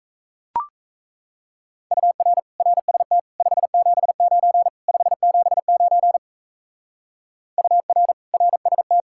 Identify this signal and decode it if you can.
Morse code, heard as a keyed tone, about 34 wpm, 700 Hz. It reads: UR RST 579 579 UR RST
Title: Speed-Racing Common QSO Phrases Encore - 28wpm